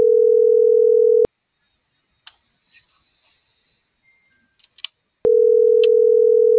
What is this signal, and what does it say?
Ringback tone (North America), 440 + 480 Hz.
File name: CENTRAL MORENA / out ruth.skyangel.24